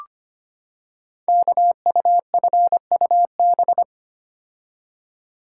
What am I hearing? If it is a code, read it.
Morse code, heard as a keyed tone, about 25 wpm, 700 Hz. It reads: KUFUB
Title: Code Groups - 5 Letters - 25wpm